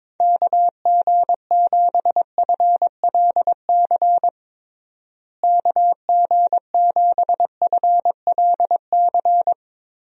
Morse code, heard as a keyed tone, about 21 wpm, 700 Hz. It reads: KG7FLC KG7FLC
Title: USA Call Signs - 6 Characters - Repeated 2x - 22wpm